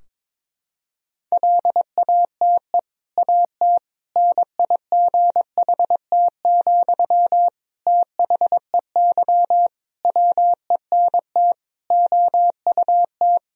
Morse code, heard as a keyed tone, about 22 wpm, 700 Hz. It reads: LATE AT NIGHT, THEY WENT OUT
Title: Sentences from Top 1000 Words - Review Part 3 - 22wpm